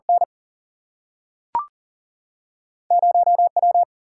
Morse code, heard as a keyed tone, about 40 wpm, 700 Hz. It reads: N 0 W
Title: Single-Letter Single-Number 40wpm